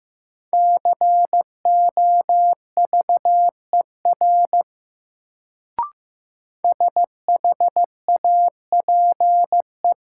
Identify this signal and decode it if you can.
Morse code, heard as a keyed tone, about 15 wpm, 700 Hz. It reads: COVER SHAPE
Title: Sentences from Top 500 Words - New Word Drill - 15wpm